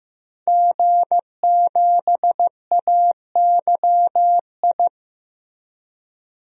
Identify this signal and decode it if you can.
Morse code, heard as a keyed tone, about 15 wpm, 700 Hz. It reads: G7AYI
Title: DXCC-100 Call Signs - 5 Characters - 15wpm